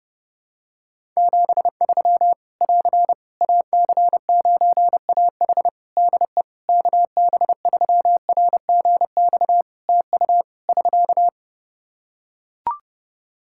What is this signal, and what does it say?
Morse code, heard as a keyed tone, about 28 wpm, 700 Hz. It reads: 73 <AR> AC9AH DE KB3RGX TU <SK>
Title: Practice QSO - Exchange 4 - Ending - 30wpm